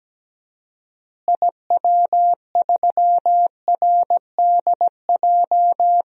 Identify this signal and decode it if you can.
Morse code, heard as a keyed tone, about 17 wpm, 700 Hz. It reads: IW3RDJ
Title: DXCC-100 Call Signs - 6 Characters - Repeated 2x - 17wpm